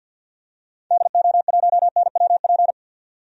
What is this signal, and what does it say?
Morse code, heard as a keyed tone, about 52 wpm, 700 Hz. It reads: DO1NWP
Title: DXCC-100 Call Signs - 6 Characters - 50wpm